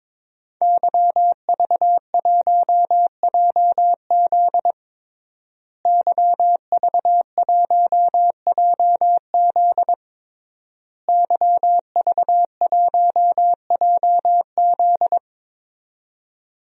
Morse code, heard as a keyed tone, about 22 wpm, 700 Hz. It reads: YV1JZ YV1JZ YV1JZ
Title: DXCC-100 Call Signs - 5 Characters - Repeated 3x - 22wpm